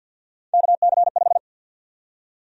Morse code, heard as a keyed tone, about 51 wpm, 700 Hz. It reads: KC5
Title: USA Call Sign Prefixes - 50wpm